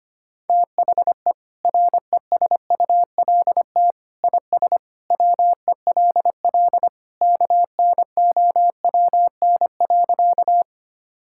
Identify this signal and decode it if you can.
Morse code, heard as a keyed tone, about 25 wpm, 700 Hz. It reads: THE RESULT IS WELL KNOWN.
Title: Sentences from Top 1000 Words - Review Part 5 - 25wpm